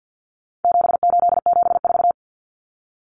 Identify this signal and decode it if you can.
Morse code, heard as a keyed tone, about 50 wpm, 700 Hz. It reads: 7874
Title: Common Cut Numbers - 4-Digit 50wpm